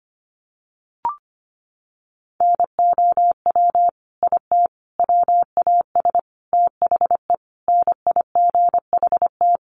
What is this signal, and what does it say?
Morse code, heard as a keyed tone, about 25 wpm, 700 Hz. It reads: NOW IT WAS THE NIGHT
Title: Sentences from Top 1400 Words 25wpm